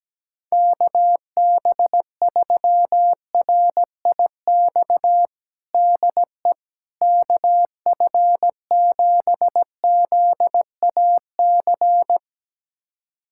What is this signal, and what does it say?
Morse code, heard as a keyed tone, about 17 wpm, 700 Hz. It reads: KB3RIX DE KF7ZAC
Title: USA Callsign DE Callsign - 6 Characters - 17wpm